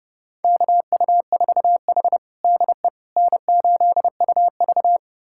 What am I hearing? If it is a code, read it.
Morse code, heard as a keyed tone, about 30 wpm, 700 Hz. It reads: KU4H DE N8UV
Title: USA Callsign DE Callsign - 4 Characters - 30wpm